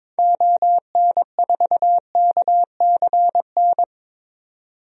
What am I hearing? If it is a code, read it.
Morse code, heard as a keyed tone, about 22 wpm, 700 Hz. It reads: ON4KCN
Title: DXCC-100 Call Signs - 6 Characters - 22wpm